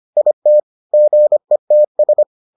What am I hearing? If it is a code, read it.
Morse code, heard as a keyed tone, about 25 wpm, 600 Hz. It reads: IT GETS